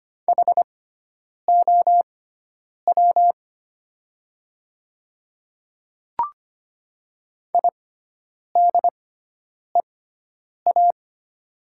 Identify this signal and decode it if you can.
Morse code, heard as a keyed tone, about 25 wpm, spaced as for 9 wpm, 700 Hz. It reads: HOW IDEA
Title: 6x Character Spacing - Top 500 Words - 25wpm